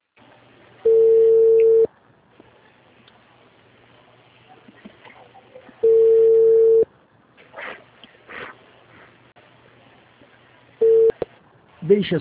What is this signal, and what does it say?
Ringback tone (China, 450 Hz), with speech.